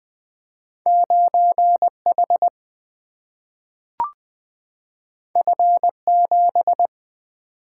Morse code, heard as a keyed tone, about 20 wpm, 700 Hz. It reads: 9 H F 7
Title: Single-Letter Single-Number 20wpm